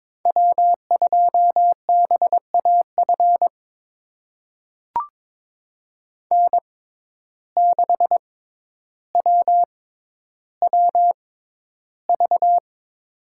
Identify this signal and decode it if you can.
Morse code, heard as a keyed tone, about 22 wpm, spaced as for 11 wpm, 700 Hz. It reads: W2BAF N6WWV
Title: USA Call Signs - 5 Characters - 6x Character Spacing - 22wpm